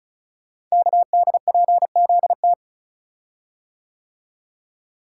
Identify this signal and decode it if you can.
Morse code, heard as a keyed tone, about 36 wpm, 700 Hz. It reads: KDPZT
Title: Code Groups - 5 Letters - 35wpm